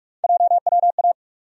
Morse code, heard as a keyed tone, about 44 wpm, 700 Hz. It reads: JWA
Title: USA Call Sign Postfixes - 3 Letters 45wpm